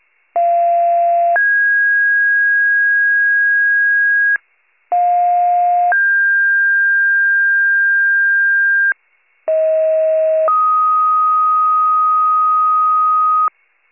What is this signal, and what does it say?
Two-tone page, heard as a one-second tone, then a three-second tone, about 690 then 1650 Hz, about 710 then 1600 Hz, about 640 then 1250 Hz. It is heard from 0.4 to 13.5 s.